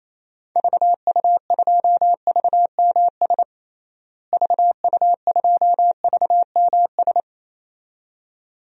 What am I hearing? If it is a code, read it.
Morse code, heard as a keyed tone, about 28 wpm, 700 Hz. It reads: VU2VMS VU2VMS
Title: DXCC-100 Call Signs - 6 Characters - Repeated 2x - 28wpm